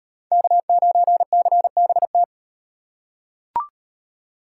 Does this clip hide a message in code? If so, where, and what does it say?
Yes: Morse code from 0.3 to 2.2 s, heard as a keyed tone, about 38 wpm, 700 Hz. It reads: K9CBT